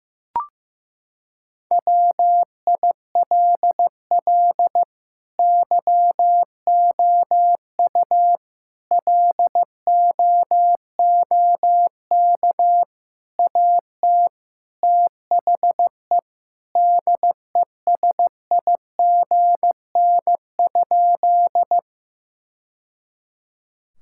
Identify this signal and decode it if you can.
Morse code, heard as a keyed tone, about 15 wpm, 700 Hz. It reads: WILL YOU LOOK AT THE DESIGN?
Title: Sentences from Top 1000 Words - Review Part 5 - 15wpm